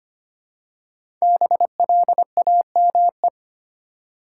Morse code, heard as a keyed tone, about 25 wpm, 700 Hz. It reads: BLAME
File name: 5 Letter Words - First 3 Letters Given - 25wpm